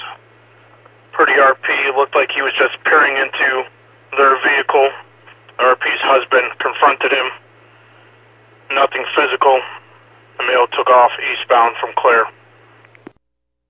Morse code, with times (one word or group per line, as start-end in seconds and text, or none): none